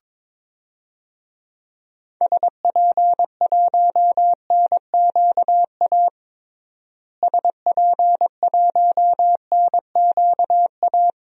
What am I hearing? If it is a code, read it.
Morse code, heard as a keyed tone, about 21 wpm, 700 Hz. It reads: SP1NQA SP1NQA
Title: DXCC-100 Call Signs - 6 Characters - Repeated 3x - 22wpm